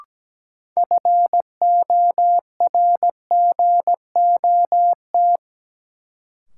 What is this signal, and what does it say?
Morse code, heard as a keyed tone, about 17 wpm, 700 Hz. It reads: FORGOT
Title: Sentences from Top 1400 Words - New Word Drill Alt - 17wpm